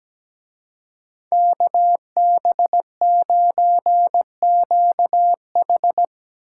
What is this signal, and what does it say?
Morse code, heard as a keyed tone, about 19 wpm, 700 Hz. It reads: KB9QH
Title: Speed Racing - USA 2x2 Call Signs 17wpm